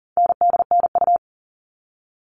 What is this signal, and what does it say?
Morse code, heard as a keyed tone, about 40 wpm, 700 Hz. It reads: NDNU
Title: All Cut Numbers - 4-Digit 40wpm